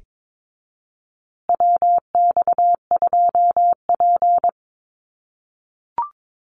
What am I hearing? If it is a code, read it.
Morse code, heard as a keyed tone, about 22 wpm, 700 Hz. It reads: WX2P
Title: USA Call Signs Repeated 3x 22wpm